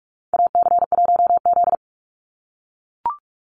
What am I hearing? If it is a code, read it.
Morse code, heard as a keyed tone, about 44 wpm, 700 Hz. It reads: AC1Z
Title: NA SOTA Chaser Callsigns 45wpm